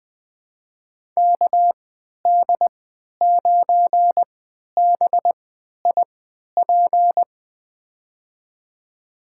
Morse code, heard as a keyed tone, about 21 wpm, spaced as for 12 wpm, 700 Hz. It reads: KD9BIP
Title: USA Call Signs - 6 Characters - 3x Character Spacing - 20wpm